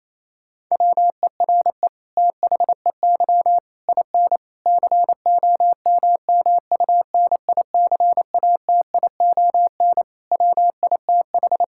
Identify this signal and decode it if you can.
Morse code, heard as a keyed tone, about 28 wpm, 700 Hz. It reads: WERE THEY IN COMMUNICATION WITH
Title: Sentences from Top 1000 Words - Review Part 6 - 28wpm